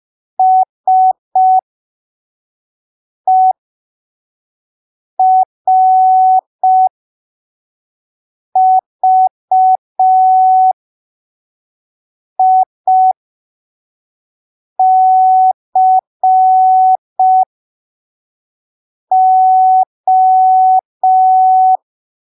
Morse code, heard as a keyed tone, about 5 wpm, spaced as for 3 wpm, 750 Hz. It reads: SERVICO
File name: arquivos / texto / texto 07 05ppm